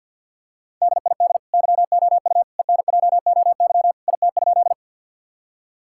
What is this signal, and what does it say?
Morse code, heard as a keyed tone, about 50 wpm, 700 Hz. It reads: DID YOU ENJOY IT?
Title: Sentences from Top 1000 Words - Review Part 5 - 50wpm